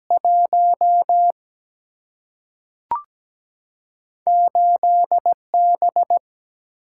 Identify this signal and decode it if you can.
Morse code, heard as a keyed tone, about 17 wpm, 700 Hz. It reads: 1 8 B
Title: Single-Letter Single-Number 17wpm